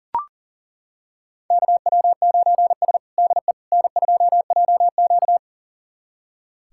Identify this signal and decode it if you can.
Morse code, heard as a keyed tone, about 37 wpm, 700 Hz. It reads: KW9S DE N2JQ
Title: USA Callsign DE Callsign - 4 Characters - 40wpm